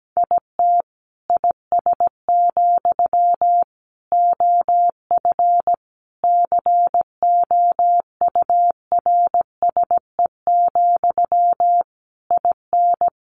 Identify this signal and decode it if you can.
Morse code, heard as a keyed tone, about 17 wpm, 700 Hz. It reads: IT IS, OF COURSE, IN
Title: Sentences from Top 1200 Words 17wpm